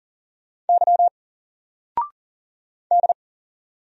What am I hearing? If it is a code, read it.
Morse code, heard as a keyed tone, about 39 wpm, 700 Hz. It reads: Y D
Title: Single Letter-Number 40wpm